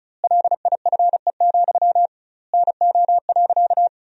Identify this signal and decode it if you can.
Morse code, heard as a keyed tone, about 34 wpm, 700 Hz. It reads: LIFE, NO.
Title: 2x Word-Spacing - Sentences from Top 300 Words - 35wpm